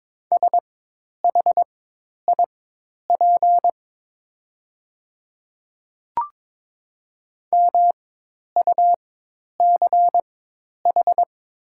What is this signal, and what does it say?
Morse code, heard as a keyed tone, about 22 wpm, spaced as for 10 wpm, 690 Hz. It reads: SHIP MUCH
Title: 4x Character Spacing - Top 500 Words - 22wpm